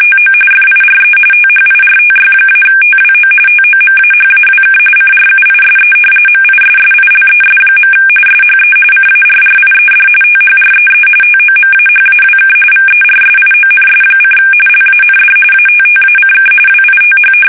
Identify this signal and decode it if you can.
RTTY, about 45.45 baud. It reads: TO FIRE THIRTY-TWO TOMAHAWK CRUISE MISSILES. SHE CARRIED THREE TOMAHAWK VERSIONS: LAND ATTACK WITH
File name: fsk850 / battleshipnj